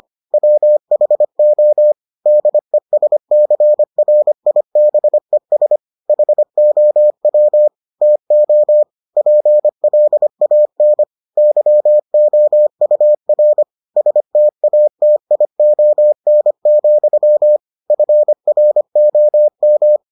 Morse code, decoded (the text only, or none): WHO DESCRIBES HOW TO PLAN YOUR STATION, FROM